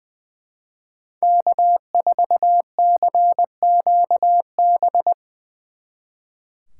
Morse code, heard as a keyed tone, about 22 wpm, 700 Hz. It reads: K4CQB
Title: Speed Racing - USA 1x3 Call Signs 20wpm